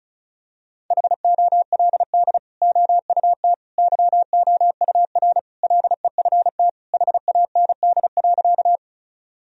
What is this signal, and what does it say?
Morse code, heard as a keyed tone, about 34 wpm, 700 Hz. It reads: HOLD OUT YOUR LEFT HAND.
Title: Sentences from Top 1000 Words - Review Part 2 - 35wpm